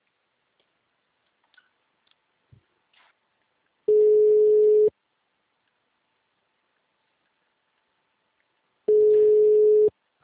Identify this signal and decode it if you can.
Ringback tone (Europe, 425 Hz).